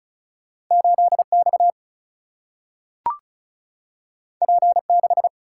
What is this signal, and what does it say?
Morse code, heard as a keyed tone, about 36 wpm, 700 Hz. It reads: 8X P6
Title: Single-Letter Single-Number 35wpm